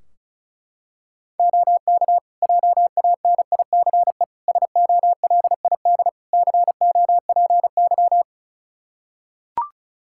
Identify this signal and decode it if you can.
Morse code, heard as a keyed tone, about 34 wpm, 700 Hz. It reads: OK JANICE SOLID COPY
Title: Practice QSO - Exchange 3 - More Info - 35wpm